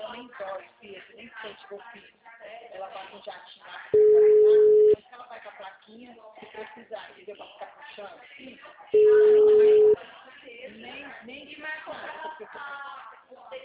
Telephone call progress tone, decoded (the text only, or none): ringback tone